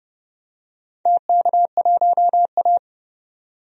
Morse code, heard as a keyed tone, about 30 wpm, 700 Hz. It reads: TK1A
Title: DXCC-100 Call Signs - 4 Characters - 30wpm